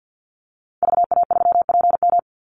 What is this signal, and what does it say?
Morse code, heard as a keyed tone, about 50 wpm, 700 Hz. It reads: VA3PM
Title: Call Signs - 50wpm